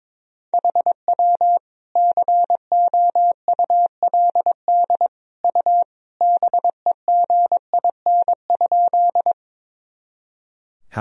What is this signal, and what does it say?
Morse code, heard as a keyed tone, about 22 wpm, 700 Hz. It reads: HW COULD U BEGIN?
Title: Sentences from Top 300 Words and Abbreviations 22wpm